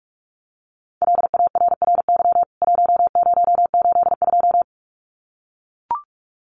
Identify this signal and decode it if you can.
Morse code, heard as a keyed tone, about 44 wpm, 700 Hz. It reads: LARRY 1082
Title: CWT Exchange 45wpm